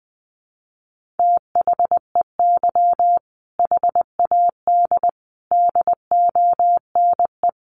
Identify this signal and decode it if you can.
Morse code, heard as a keyed tone, about 20 wpm, 700 Hz. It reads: THEY HAD DONE
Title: Sentences from Top 1500 Words 20wpm